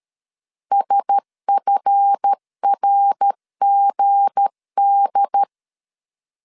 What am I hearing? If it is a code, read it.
Morse code, heard as a keyed tone, about 13 wpm, 800 Hz. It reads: SFRGD
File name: LEZIONI / 7 GF9Z / 7 GF 12wpm 2122